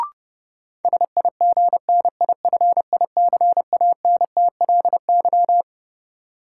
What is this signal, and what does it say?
Morse code, heard as a keyed tone, about 32 wpm, 700 Hz. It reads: SIGNIFICANTLY
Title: Sentences from Top 1500 Words - New Word Drill Alt - 30wpm